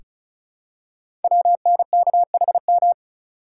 Morse code, heard as a keyed tone, about 36 wpm, 700 Hz. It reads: WNKHM
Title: Code Groups - 5 Letters - 35wpm